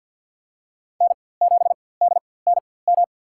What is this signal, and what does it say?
Morse code, heard as a keyed tone, about 50 wpm, spaced as for 23 wpm, 700 Hz. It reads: N7DNM
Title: USA Call Signs - 5 Characters - 4x Character Spacing - 50wpm